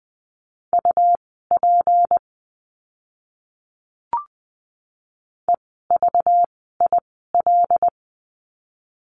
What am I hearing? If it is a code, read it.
Morse code, heard as a keyed tone, about 20 wpm, spaced as for 14 wpm, 700 Hz. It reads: UP EVIL
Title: 192 Lesson 31f - Letter v - Words 20wpm